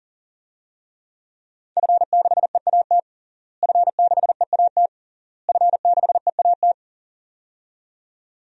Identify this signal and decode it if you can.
Morse code, heard as a keyed tone, about 37 wpm, 700 Hz. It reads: F6EAT F6EAT F6EAT
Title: DXCC-100 Call Signs - 5 Characters - Repeated 3x - 40wpm